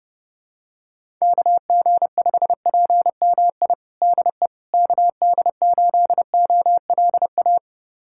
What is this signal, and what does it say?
Morse code, heard as a keyed tone, about 32 wpm, 700 Hz. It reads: KG5PMI DE KD8OLA
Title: USA Callsign DE Callsign - 6 Characters - 30wpm